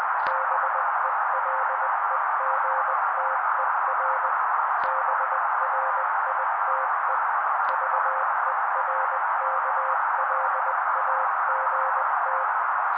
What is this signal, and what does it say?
Morse code, heard as a keyed tone, about 20 wpm, 530 Hz. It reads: BELEGTER BRITE VERKLAGT
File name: daily / 18wpm noise